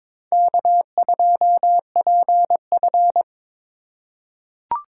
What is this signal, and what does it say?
Morse code, heard as a keyed tone, about 22 wpm, 700 Hz. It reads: K2PF
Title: USA Call Signs - 4 Characters - 3x Character Spacing - 22wpm